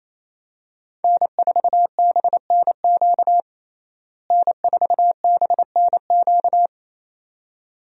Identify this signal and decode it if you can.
Morse code, heard as a keyed tone, about 27 wpm, 700 Hz. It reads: N4BNQ N4BNQ
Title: USA Call Signs - 5 Characters - Repeated 2x - 28wpm